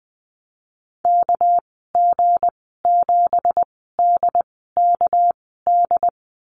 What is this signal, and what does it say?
Morse code, heard as a keyed tone, about 20 wpm, spaced as for 14 wpm, 700 Hz. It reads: KG7DKD
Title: 245 Lesson 39g - Letter z - Call Signs 20wpm